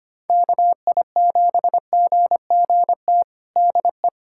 Morse code, heard as a keyed tone, about 25 wpm, 700 Hz. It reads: KI7GGT DE
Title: USA Callsign DE Callsign - 6 Characters - 25wpm